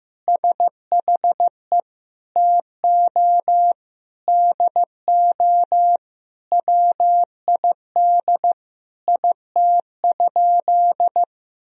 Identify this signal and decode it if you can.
Morse code, heard as a keyed tone, about 16 wpm, 700 Hz. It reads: SHE TO DO WID IT?
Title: Sentences from Top 100 Words and Abbreviations 15wpm